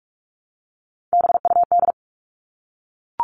Morse code, heard as a keyed tone, about 44 wpm, 700 Hz. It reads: BUD
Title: All Cut Numbers - 3-Digit 45wpm